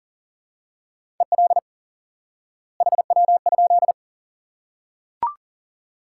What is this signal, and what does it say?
Morse code, heard as a keyed tone, about 40 wpm, 700 Hz. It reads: EL HW?
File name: IQE Course 3 - Lesson 89 - YAGI - Two Elements - 40wpm